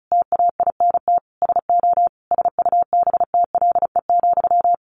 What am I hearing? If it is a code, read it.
Morse code, heard as a keyed tone, about 34 wpm, 700 Hz. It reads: TAINT SO SUBTLE,
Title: A Sound Of Thunder 35wpm